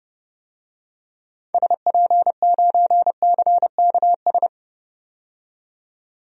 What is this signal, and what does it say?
Morse code, heard as a keyed tone, about 32 wpm, 700 Hz. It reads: SP9CKS
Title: DXCC-100 Call Signs - 6 Characters - 30wpm